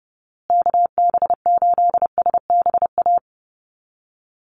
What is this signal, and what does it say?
Morse code, heard as a keyed tone, about 30 wpm, 700 Hz. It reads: KB8SBA
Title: USA Call Signs Repeated 3x 30wpm